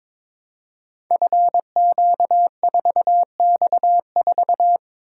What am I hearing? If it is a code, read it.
Morse code, heard as a keyed tone, about 22 wpm, 700 Hz. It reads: FQ4X4
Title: Code Groups - 5 Letter:Numbers - 22wpm